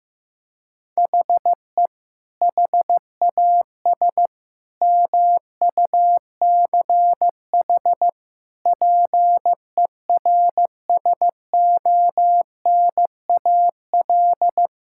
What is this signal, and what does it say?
Morse code, heard as a keyed tone, about 15 wpm, 700 Hz. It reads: HE HAS MUCH PERSONAL
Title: Sentences from Top 1000 Words - Review Part 5 - 15wpm